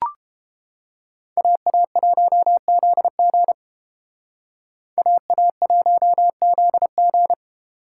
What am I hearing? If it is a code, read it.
Morse code, heard as a keyed tone, about 30 wpm, 700 Hz. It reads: AA1ZG AA1ZG